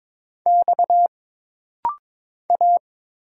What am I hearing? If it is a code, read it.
Morse code, heard as a keyed tone, about 22 wpm, 700 Hz. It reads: X A
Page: This rapid-fire timing is geared to push your ICR (Instant Character Recognition) skills.